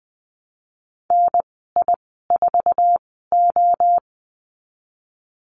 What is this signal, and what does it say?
Morse code, heard as a keyed tone, about 20 wpm, spaced as for 15 wpm, 700 Hz. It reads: NI4O
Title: 048 Lesson 9g - Number 4 - Call Signs 20wpm